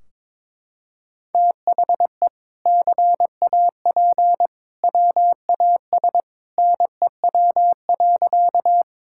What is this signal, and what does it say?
Morse code, heard as a keyed tone, about 22 wpm, 700 Hz. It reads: THE CAP WAS NEW.
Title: Sentences from Top 1600 Words 22wpm